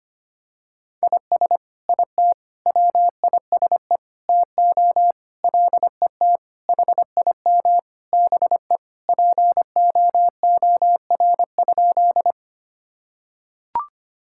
Morse code, heard as a keyed tone, about 25 wpm, 700 Hz. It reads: IS IT WISE TO LET HIM BE POOR?
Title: Sentences from Top 1600 Words 25wpm